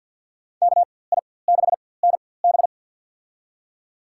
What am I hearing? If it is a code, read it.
Morse code, heard as a keyed tone, about 49 wpm, spaced as for 23 wpm, 700 Hz. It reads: KI6NB
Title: USA Call Signs - 5 Characters - 4x Character Spacing - 50wpm